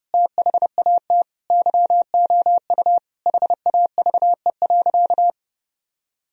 Morse code, heard as a keyed tone, about 30 wpm, 700 Hz. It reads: THAT YOU HAVE.
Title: Sentences from Top 1000 Words - Review Part 4 - 30wpm